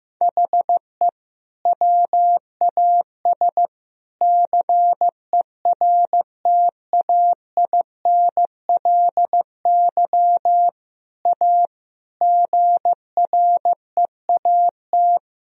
Morse code, heard as a keyed tone, about 15 wpm, 700 Hz. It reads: HE WAS CERTAINLY A GREAT
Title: Sentences from Top 1000 Words - Review Part 1 - 15wpm